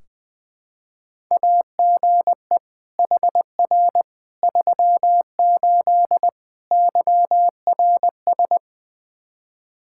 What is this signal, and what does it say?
Morse code, heard as a keyed tone, about 20 wpm, 700 Hz. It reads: AGE HR 38 YRS
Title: Practice QSO - Exchange 3 - More Info - 20wpm